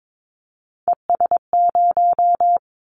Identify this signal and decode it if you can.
Morse code, heard as a keyed tone, about 22 wpm, 700 Hz. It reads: ES0
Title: Callsign Prefixes for Top 100 DXCC - 22wpm